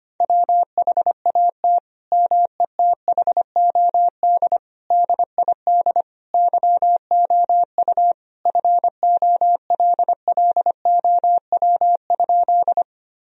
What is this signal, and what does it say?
Morse code, heard as a keyed tone, about 25 wpm, 700 Hz. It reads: WHAT METHOD DID YOU FOLLOW?
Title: Sentences from Top 1000 Words - Review Part 3 - 25wpm